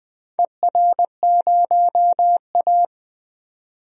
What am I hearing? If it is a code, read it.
Morse code, heard as a keyed tone, about 20 wpm, 700 Hz. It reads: ER0A
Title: DXCC-100 Call Signs - 4 Characters - 20wpm